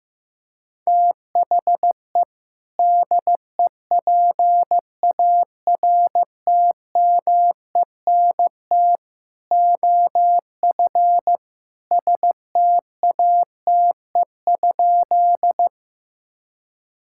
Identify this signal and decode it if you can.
Morse code, heard as a keyed tone, about 16 wpm, 700 Hz. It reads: THE DEPARTMENT OF STATE?
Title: Sentences from Top 1000 Words - Review Part 5 - 15wpm